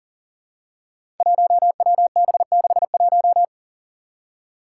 Morse code, heard as a keyed tone, about 41 wpm, 700 Hz. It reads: 1WB61
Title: Five Challenging Letter-Number Combinations 40wpm